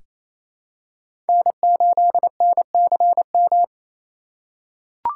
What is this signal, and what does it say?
Morse code, heard as a keyed tone, about 28 wpm, 700 Hz. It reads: N8NCM
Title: USA Call Signs - 5 Characters - 4x Character Spacing - 28wpm